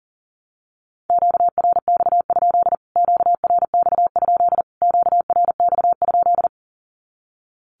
Morse code, heard as a keyed tone, about 40 wpm, 700 Hz. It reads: QRX? QRX? QRX?
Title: Q-Codes Repeated 3x 40wpm